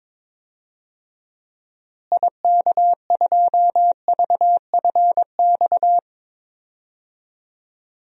Morse code, heard as a keyed tone, about 21 wpm, 700 Hz. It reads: IK2VFX
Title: DXCC-100 Call Signs - 6 Characters - Repeated 2x - 22wpm